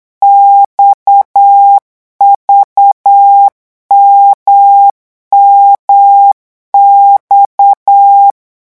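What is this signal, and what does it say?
Morse code, heard as a keyed tone, about 8 wpm, 800 Hz. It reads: XVMMX
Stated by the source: Morse code revision lesson at 8 words per minute, from a French amateur radio course.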